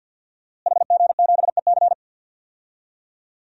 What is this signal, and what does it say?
Morse code, heard as a keyed tone, about 52 wpm, 700 Hz. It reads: HG7EC
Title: DXCC-100 Call Signs - 5 Characters - Repeated 3x - 50wpm